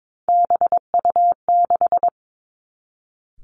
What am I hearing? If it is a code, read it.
Morse code, heard as a keyed tone, about 22 wpm, 700 Hz. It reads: BU6
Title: Three Challenging Letter-Number Combinations 22wpm